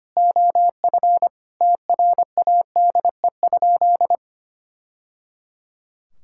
Morse code, heard as a keyed tone, about 25 wpm, 700 Hz. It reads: OF TRADE?
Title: Sentences from Top 1000 Words - Review Part 6 - 25wpm